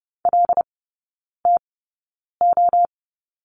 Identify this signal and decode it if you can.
Morse code, heard as a keyed tone, about 30 wpm, spaced as for 9 wpm, 700 Hz. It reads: LTO
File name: Lesson 16e - Letter c - Three Characters 30wpm